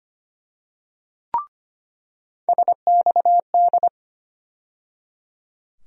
Morse code, heard as a keyed tone, about 25 wpm, 700 Hz. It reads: SXD
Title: USA Call Sign Postfixes - 3 Letters 25wpm